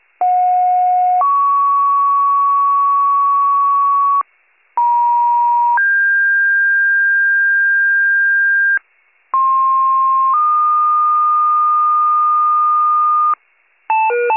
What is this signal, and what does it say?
Two-tone page, heard as a one-second tone, then a three-second tone, about 730 then 1120 Hz, about 950 then 1600 Hz, about 1060 then 1220 Hz. It is heard from 0.2 to 13.3 s.